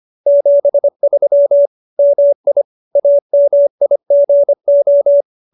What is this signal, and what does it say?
Morse code, heard as a keyed tone, about 25 wpm, 570 Hz. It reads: 73 MI AMIGO